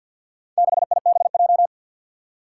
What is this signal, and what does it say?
Morse code, heard as a keyed tone, about 53 wpm, 700 Hz. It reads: 6IBJ
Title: Four Challenging Letter-Number Combinations 50wpm